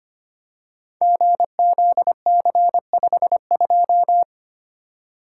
Morse code, heard as a keyed tone, about 25 wpm, 700 Hz. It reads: GZC52
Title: Code Groups - 5 Letter:Numbers - 25wpm